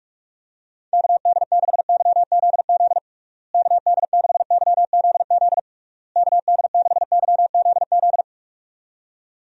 Morse code, heard as a keyed tone, about 46 wpm, 700 Hz. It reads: KD6YZZ KD6YZZ KD6YZZ
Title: USA Call Signs - 6 Characters - Repeated 3x - 45wpm